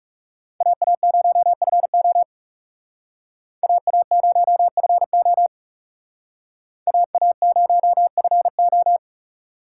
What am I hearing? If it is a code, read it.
Morse code, heard as a keyed tone, about 38 wpm, 700 Hz. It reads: AA0FO AA0FO AA0FO